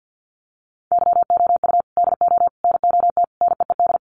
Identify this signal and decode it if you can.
Morse code, heard as a keyed tone, about 50 wpm, 700 Hz. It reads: YOU DO NOT NEED
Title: Sets of 4 Words 50wpm